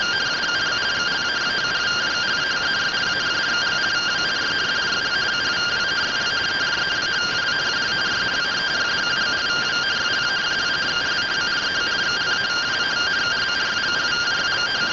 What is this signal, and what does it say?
Two-tone teleprinter signal, about 170 Hz shift.
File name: findings / AXIAL fr1 3mm.gen-05 rtty rx